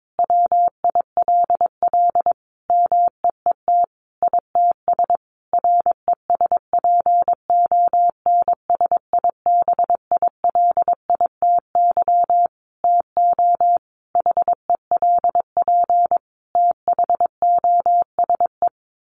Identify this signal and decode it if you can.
Morse code, heard as a keyed tone, about 22 wpm, 700 Hz. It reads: WILL MEET ITS RESPONSIBILITY TO HELP THOSE
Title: Sentences from Top 1200 Words 22wpm